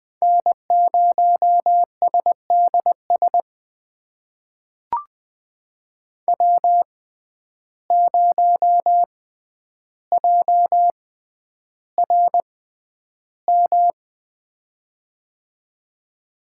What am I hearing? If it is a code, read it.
Morse code, heard as a keyed tone, about 21 wpm, spaced as for 10 wpm, 700 Hz. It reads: N0SDS W0JRM
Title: USA Call Signs - 5 Characters - 6x Character Spacing - 20wpm